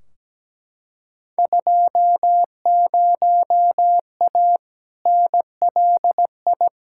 Morse code, heard as a keyed tone, about 17 wpm, 700 Hz. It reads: 20A NLI
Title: Field Day Exchanges Encore - 17wpm